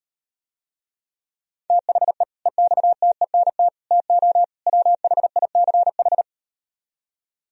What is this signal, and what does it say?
Morse code, heard as a keyed tone, about 38 wpm, 700 Hz. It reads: THE EXTENT TO WHICH